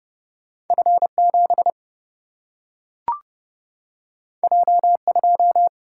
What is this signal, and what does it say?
Morse code, heard as a keyed tone, about 30 wpm, 700 Hz. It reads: F 7 J 2
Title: Single-Letter Single-Number 30wpm